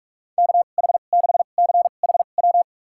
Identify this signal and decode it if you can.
Morse code, heard as a keyed tone, about 41 wpm, spaced as for 31 wpm, 700 Hz. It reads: KH6CHW
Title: USA Call Signs - 6 Characters - 2x Character Spacing - 45wpm